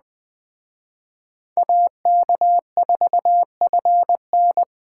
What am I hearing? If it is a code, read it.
Morse code, heard as a keyed tone, about 20 wpm, 700 Hz. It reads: AK4FN
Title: Speed Racing - USA 2x2 Call Signs 17wpm